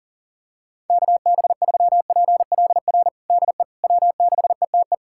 Morse code, heard as a keyed tone, about 40 wpm, 700 Hz. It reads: KB3PLR DE W6ETE
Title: USA Callsign DE Callsign 40wpm